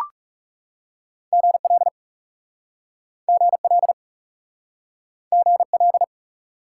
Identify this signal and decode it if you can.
Morse code, heard as a keyed tone, about 41 wpm, 700 Hz. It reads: GL GL GL